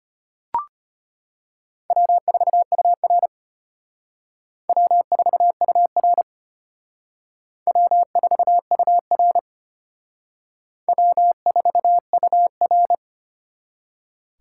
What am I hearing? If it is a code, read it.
Morse code, heard as a keyed tone, about 32 wpm, 700 Hz. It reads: W4UR W4UR W4UR W4UR